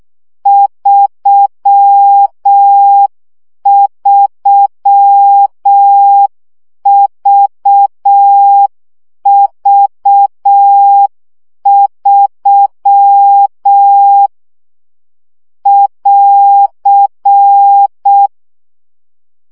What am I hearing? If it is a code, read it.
Morse code, heard as a keyed tone, about 6 wpm, 800 Hz. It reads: 33VV3 <AR>
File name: CW gyakorlas v 3 5 baud